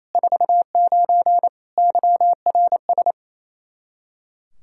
Morse code, heard as a keyed tone, about 27 wpm, 700 Hz. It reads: 49 YRS
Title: K1USN SST QSO Practice 28wpm